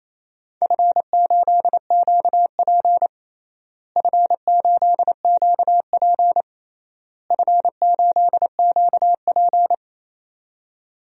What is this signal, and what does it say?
Morse code, heard as a keyed tone, about 28 wpm, 700 Hz. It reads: F8QP F8QP F8QP
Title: DXCC-100 Call Signs - 4 Characters - Repeated 3x - 28wpm